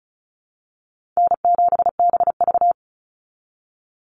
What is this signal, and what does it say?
Morse code, heard as a keyed tone, about 34 wpm, 700 Hz. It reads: N7BV
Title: Call Signs Repeated 2x - 35wpm